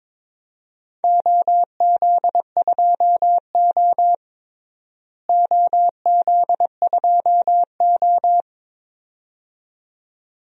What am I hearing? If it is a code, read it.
Morse code, heard as a keyed tone, about 22 wpm, 700 Hz. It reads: OZ2O OZ2O
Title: DXCC-100 Call Signs - 4 Characters - Repeated 2x - 22wpm